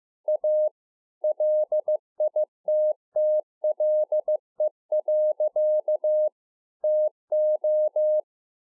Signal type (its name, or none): Morse code